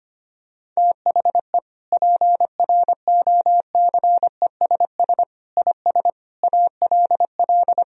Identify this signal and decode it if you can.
Morse code, heard as a keyed tone, about 25 wpm, 700 Hz. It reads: THE PROCESS IS ALL
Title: Sentences from Top 1000 Words - Review Part 3 - 25wpm